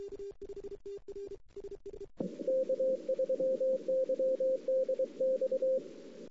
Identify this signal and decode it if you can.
Morse code, heard as a keyed tone, about 23 wpm, 520 Hz. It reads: K3YDX
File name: CW / 2008-ss-cw-k5tr-op-n5rz -03:00